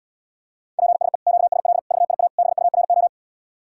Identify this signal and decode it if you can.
Morse code, heard as a keyed tone, about 79 wpm, 700 Hz. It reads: PSE QSL VIA BURO